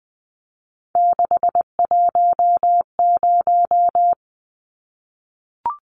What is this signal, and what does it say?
Morse code, heard as a keyed tone, about 20 wpm, 700 Hz. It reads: 610
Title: Numbers - 3-Digit 20wpm